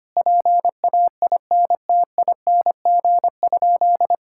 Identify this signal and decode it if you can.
Morse code, heard as a keyed tone, about 25 wpm, 700 Hz. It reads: PAINTING?
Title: Sentences from Top 1000 Words - Review Part 6 - 25wpm